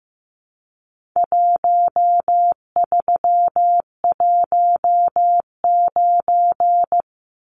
Morse code, heard as a keyed tone, about 15 wpm, 700 Hz. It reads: 1319
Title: Numbers - Contest Style - 4-Digit 15wpm